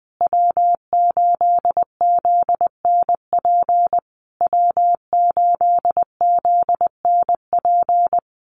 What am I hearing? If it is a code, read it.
Morse code, heard as a keyed tone, about 20 wpm, 700 Hz. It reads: W8ZNP W8ZNP
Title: USA Call Signs Repeated 3x 20wpm